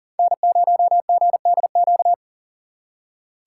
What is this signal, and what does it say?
Morse code, heard as a keyed tone, about 51 wpm, 700 Hz. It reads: N0GDQ